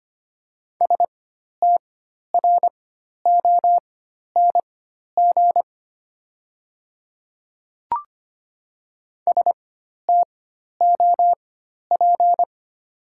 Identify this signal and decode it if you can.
Morse code, heard as a keyed tone, about 25 wpm, spaced as for 12 wpm, 700 Hz. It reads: STRONG STOP
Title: 4x Character Spacing - Top 500 Words - 25wpm